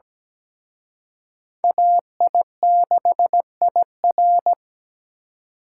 Morse code, heard as a keyed tone, about 20 wpm, 700 Hz. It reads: AI6IR
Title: Speed Racing - USA 2x2 Call Signs 17wpm